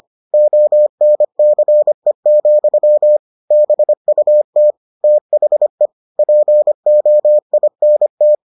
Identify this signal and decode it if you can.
Morse code, heard as a keyed tone, about 25 wpm, 600 Hz. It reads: ONCE, BUT THE POINT